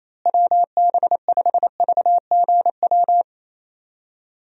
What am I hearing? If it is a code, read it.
Morse code, heard as a keyed tone, about 28 wpm, 700 Hz. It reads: WB5VGW
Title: USA Call Signs - 6 Characters - Repeated 3x - 28wpm